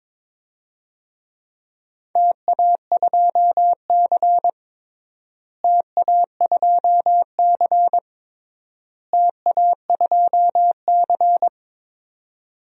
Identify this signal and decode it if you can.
Morse code, heard as a keyed tone, about 22 wpm, 700 Hz. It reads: TA2C TA2C TA2C
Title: DXCC-100 Call Signs - 4 Characters - Repeated 3x - 22wpm